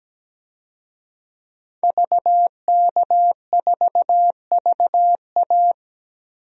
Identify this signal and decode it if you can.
Morse code, heard as a keyed tone, about 17 wpm, 700 Hz. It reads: VK4VA
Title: DXCC-100 Call Signs - 5 Characters - Repeated 3x - 17wpm